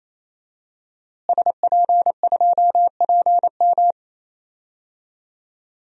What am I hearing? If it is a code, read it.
Morse code, heard as a keyed tone, about 28 wpm, 700 Hz. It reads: SP2PM
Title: DXCC-100 Call Signs - 5 Characters - Repeated 3x - 28wpm